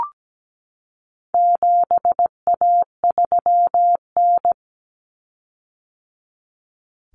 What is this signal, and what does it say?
Morse code, heard as a keyed tone, about 17 wpm, 700 Hz. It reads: 7A3N
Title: Common Cut Numbers - 4-Digit 17wpm